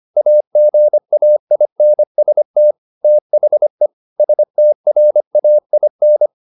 Morse code, heard as a keyed tone, about 25 wpm, 600 Hz. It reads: AGAINST THE STRAIN
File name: the old man and the sea part 1 25wpm